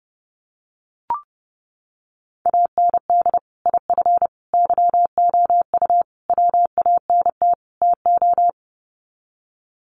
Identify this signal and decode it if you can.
Morse code, heard as a keyed tone, about 30 wpm, 700 Hz. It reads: AND IF YOU WANT TO
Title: Sets of 5 Words Encore 30wpm